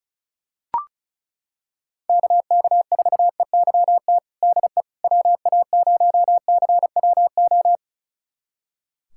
Morse code, heard as a keyed tone, about 34 wpm, 700 Hz. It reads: KK4EYT DE WA0CWO
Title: USA Callsign DE Callsign - 6 Characters - 35wpm